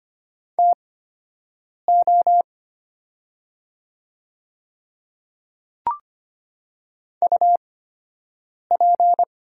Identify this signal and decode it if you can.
Morse code, heard as a keyed tone, about 25 wpm, spaced as for 7 wpm, 700 Hz. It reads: TO UP
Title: 8x Character Spacing - Top 100 Words - 25wpm